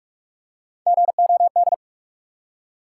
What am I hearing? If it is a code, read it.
Morse code, heard as a keyed tone, about 46 wpm, 700 Hz. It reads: GOD
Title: Three Letter Words - 45wpm